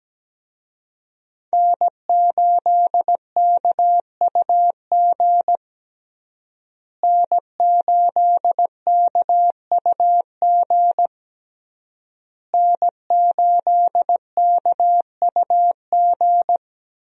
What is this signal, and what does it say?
Morse code, heard as a keyed tone, about 17 wpm, 700 Hz. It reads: N8KUG N8KUG N8KUG
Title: USA Call Signs - 5 Characters - Repeated 3x - 17wpm